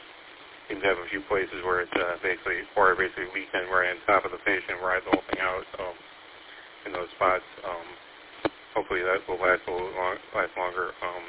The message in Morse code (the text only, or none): none